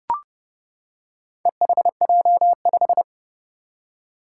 Morse code, heard as a keyed tone, about 30 wpm, 700 Hz. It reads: EHJ5
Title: Four Challenging Letter-Number Combinations 30wpm